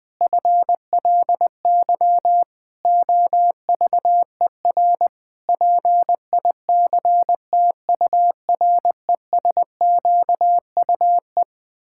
Morse code, heard as a keyed tone, about 21 wpm, 700 Hz. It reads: FLY OVER PICTURESQUE
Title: Pangram Challenge v2 - 20wpm